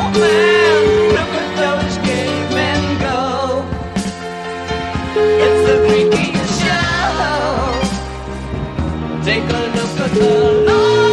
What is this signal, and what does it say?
Ringback tone (Europe, 425 Hz), with noise.